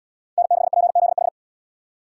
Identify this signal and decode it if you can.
Morse code, heard as a keyed tone, about 77 wpm, 700 Hz. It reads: N6PZH